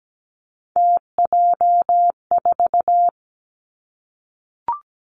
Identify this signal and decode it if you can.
Morse code, heard as a keyed tone, about 17 wpm, 700 Hz. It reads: TJ4
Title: Three Challenging Letter-Number Combinations 17wpm